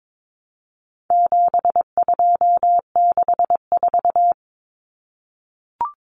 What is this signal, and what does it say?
Morse code, heard as a keyed tone, about 22 wpm, 700 Hz. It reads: 7264
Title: Numbers - 4-Digit 22wpm